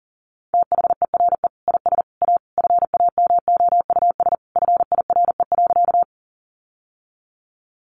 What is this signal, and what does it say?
Morse code, heard as a keyed tone, about 40 wpm, 700 Hz. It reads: THERE IS A FAMOUS FIRE.
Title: Sentences from Top 1500 Words 40wpm